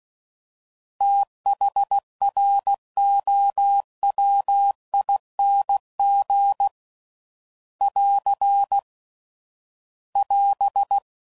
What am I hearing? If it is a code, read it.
Morse code, heard as a keyed tone, about 16 wpm, 800 Hz. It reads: THROWING <AR> <AS>
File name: Prefixes / FLs & THs / Fast TH